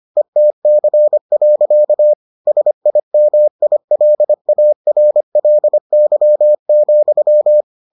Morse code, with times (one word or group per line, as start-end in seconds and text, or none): 0.2-2.1 ETC.
2.5-7.6 SIMILARLY,